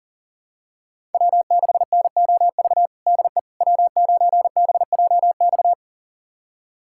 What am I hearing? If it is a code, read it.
Morse code, heard as a keyed tone, about 40 wpm, 700 Hz. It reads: W6NOV DE W9BJX
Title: USA Callsign DE Callsign - 5 Characters - 40wpm